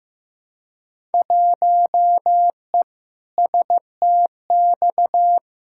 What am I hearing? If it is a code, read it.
Morse code, heard as a keyed tone, about 15 wpm, 700 Hz. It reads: 1E STX
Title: Field Day Exchanges - 15wpm